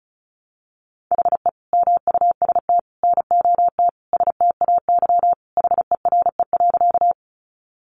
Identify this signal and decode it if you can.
Morse code, heard as a keyed tone, about 34 wpm, 700 Hz. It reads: HE MUST NOT STAY HERE.
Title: Sentences from Top 600 Words 35wpm